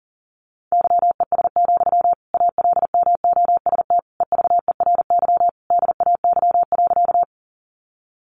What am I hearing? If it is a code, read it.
Morse code, heard as a keyed tone, about 40 wpm, 700 Hz. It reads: YES, ALMOST EVERY DAY.
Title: Sentences from Top 700 Words 40wpm